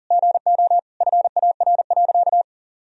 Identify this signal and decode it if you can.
Morse code, heard as a keyed tone, about 37 wpm, 700 Hz. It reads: GO FAR.
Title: Sentences from Top 400 Words and Abbreviations 40wpm